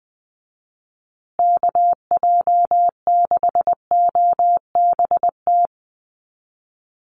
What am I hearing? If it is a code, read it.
Morse code, heard as a keyed tone, about 20 wpm, 700 Hz. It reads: KJ6OBT
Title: USA Call Signs Repeated 3x 20wpm